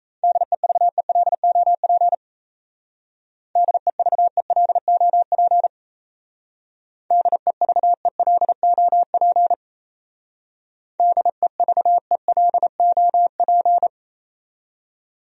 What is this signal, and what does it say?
Morse code, heard as a keyed tone, about 33 wpm, 700 Hz. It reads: DEVELOP DEVELOP DEVELOP DEVELOP
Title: Speed-Racing Top 500 Words 28wpm